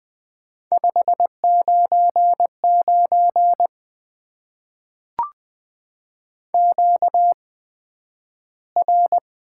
Morse code, heard as a keyed tone, about 20 wpm, spaced as for 11 wpm, 700 Hz. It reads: 599 QR
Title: Top 90 QSO Elements - 8x Character Spacing - 20wpm